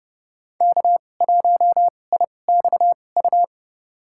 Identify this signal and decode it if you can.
Morse code, heard as a keyed tone, about 30 wpm, spaced as for 22 wpm, 700 Hz. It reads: K1IXU
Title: USA Call Signs - 5 Characters - 2x Character Spacing - 30wpm